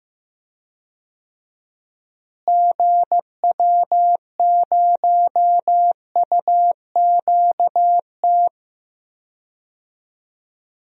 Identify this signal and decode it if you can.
Morse code, heard as a keyed tone, about 15 wpm, 700 Hz. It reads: GW0UQT
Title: DXCC-100 Call Signs - 6 Characters - Repeated 2x - 15wpm